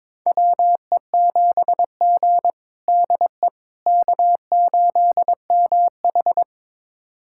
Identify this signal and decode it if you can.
Morse code, heard as a keyed tone, about 22 wpm, 700 Hz. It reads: WE7G DE K8MH
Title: USA Callsign DE Callsign - 4 Characters - 22wpm